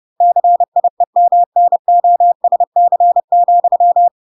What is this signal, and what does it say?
Morse code, heard as a keyed tone, about 30 wpm, 700 Hz. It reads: CIEMNOSC,